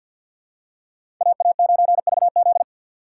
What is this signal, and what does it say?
Morse code, heard as a keyed tone, about 54 wpm, 700 Hz. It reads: AA9VZ
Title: USA Call Signs - 5 Characters - 8x Character Spacing - 50wpm